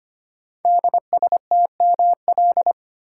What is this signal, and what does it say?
Morse code, heard as a keyed tone, about 25 wpm, 700 Hz. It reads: DSTML
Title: Code Groups - 5 Letters - 25wpm